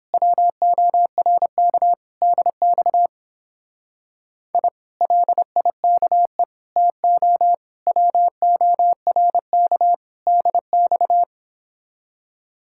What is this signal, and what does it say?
Morse code, heard as a keyed tone, about 29 wpm, 700 Hz. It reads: WORK DX I LIKE TO WORK DX